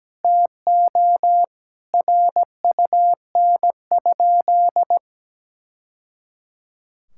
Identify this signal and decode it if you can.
Morse code, heard as a keyed tone, about 17 wpm, 700 Hz. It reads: TO RUN?
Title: Sentences from Top 1000 Words - Review Part 6 - 17wpm